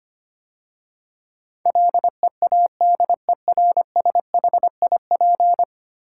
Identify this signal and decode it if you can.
Morse code, heard as a keyed tone, about 25 wpm, 700 Hz. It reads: LEADERSHIP
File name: 10 Letter Words - First 3 Letters Given - 25wpm